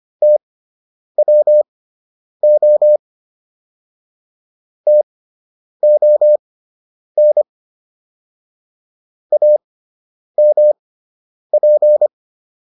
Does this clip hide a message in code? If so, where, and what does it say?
Yes: Morse code from 0.2 to 12.1 s, heard as a keyed tone, about 25 wpm, spaced as for 9 wpm, 600 Hz. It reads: TWO TON AMP